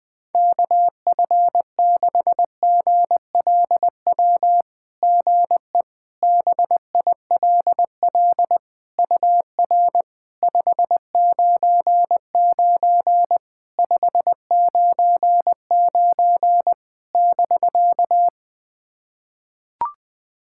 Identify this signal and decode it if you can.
Morse code, heard as a keyed tone, about 21 wpm, 660 Hz. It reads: KF6GLW GE BILL UR 599 599 <BK>
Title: SOTA QSO Practice 20wpm